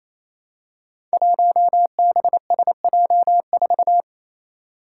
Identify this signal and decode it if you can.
Morse code, heard as a keyed tone, about 27 wpm, 700 Hz. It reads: 1BSJ4
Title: Five Challenging Letter-Number Combinations 28wpm